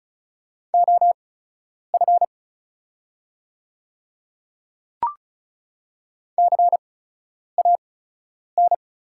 Morse code, heard as a keyed tone, about 36 wpm, spaced as for 10 wpm, 690 Hz. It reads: OF CAN
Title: 8x Character Spacing - Top 100 Words - 35wpm